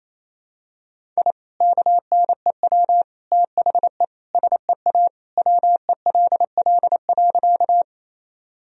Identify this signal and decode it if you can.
Morse code, heard as a keyed tone, about 28 wpm, 700 Hz. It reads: I KNEW THE SEA WELL.
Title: Sentences from Top 1000 Words - Review Part 1 - 28wpm